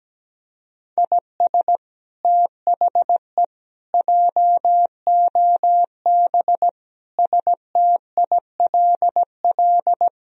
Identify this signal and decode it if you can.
Morse code, heard as a keyed tone, about 17 wpm, 700 Hz. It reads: IS THE JOB STILL
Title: Sentences from Top 1000 Words - Review Part 3 - 17wpm